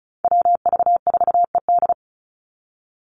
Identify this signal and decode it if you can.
Morse code, heard as a keyed tone, about 34 wpm, 700 Hz. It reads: WV4ED
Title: USA Call Signs 35wpm